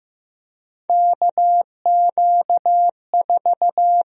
Morse code, heard as a keyed tone, about 15 wpm, 700 Hz. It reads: KQ4
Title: USA Call Sign Prefixes - 15wpm